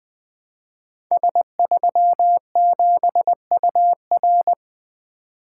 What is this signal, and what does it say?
Morse code, heard as a keyed tone, about 20 wpm, 700 Hz. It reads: S37UR
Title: Code Groups - 5 Letter:Numbers - 20wpm